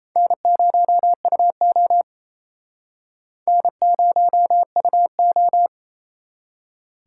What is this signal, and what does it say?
Morse code, heard as a keyed tone, about 37 wpm, 700 Hz. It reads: N0UO N0UO